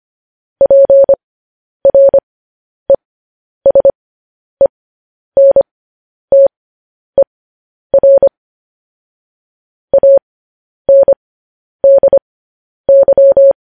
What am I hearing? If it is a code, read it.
Morse code, heard as a keyed tone, about 25 wpm, spaced as for 10 wpm, 550 Hz. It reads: PRESENTER ANDY
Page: news-headlines-in-morse-code-at-10-wpm